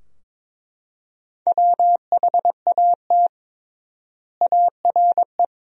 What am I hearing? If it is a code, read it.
Morse code, heard as a keyed tone, about 22 wpm, 700 Hz. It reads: WHAT ARE
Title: 3x Word-Spacing - Sentences from Top 100 Words - 22wpm